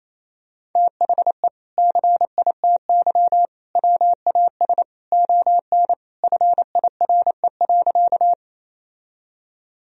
Morse code, heard as a keyed tone, about 27 wpm, 700 Hz. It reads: THE CITY WAS ON FIRE.
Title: Sentences from Top 1000 Words - Review Part 1 - 28wpm